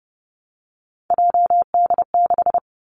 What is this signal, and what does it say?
Morse code, heard as a keyed tone, about 30 wpm, 700 Hz. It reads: JD6
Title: Three Challenging Letter-Number Combinations 30wpm